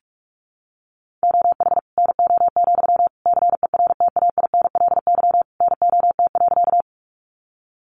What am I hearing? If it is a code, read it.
Morse code, heard as a keyed tone, about 44 wpm, 700 Hz. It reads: OH NO, CERTAINLY NOT.
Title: Sentences from Top 900 Words 45wpm